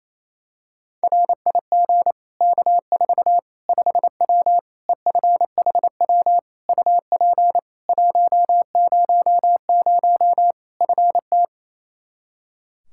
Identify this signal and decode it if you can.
Morse code, heard as a keyed tone, about 28 wpm, 700 Hz. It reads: RIG K4 5W EFHW UP 100 FT
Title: Practice QSO - Exchange 2 - Info - 28wpm